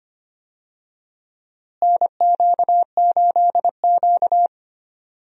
Morse code, heard as a keyed tone, about 25 wpm, 700 Hz. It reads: NQ8Q
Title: USA Call Signs - 4 Characters - Repeated 2x - 25wpm